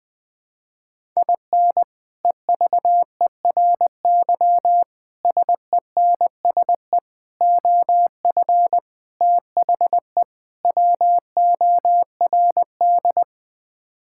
Morse code, heard as a keyed tone, about 20 wpm, 700 Hz. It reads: IN EVERY SENSE OF THE WORD
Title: Sets of 6 Words - 20wpm